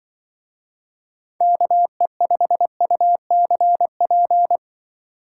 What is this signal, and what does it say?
Morse code, heard as a keyed tone, about 25 wpm, 700 Hz. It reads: KE5UCP
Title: Speed Racing - USA Call Signs 20wpm